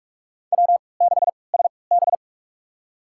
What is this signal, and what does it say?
Morse code, heard as a keyed tone, about 46 wpm, spaced as for 25 wpm, 700 Hz. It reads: W6SB
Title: USA Call Signs - 4 Characters - 3x Character Spacing - 45wpm